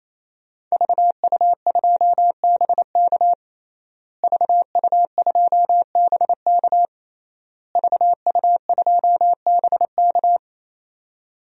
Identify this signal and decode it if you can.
Morse code, heard as a keyed tone, about 27 wpm, 700 Hz. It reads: VU2BK VU2BK VU2BK
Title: DXCC-100 Call Signs - 5 Characters - Repeated 3x - 28wpm